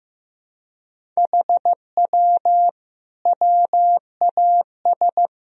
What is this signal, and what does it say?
Morse code, heard as a keyed tone, about 15 wpm, 700 Hz. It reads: HW WAS
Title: Sentences from Top 100 Words and Abbreviations 15wpm